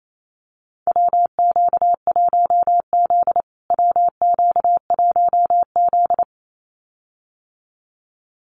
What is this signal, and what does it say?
Morse code, heard as a keyed tone, about 28 wpm, 700 Hz. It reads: WQ1Z WQ1Z
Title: USA Call Signs Repeated 2x 28wpm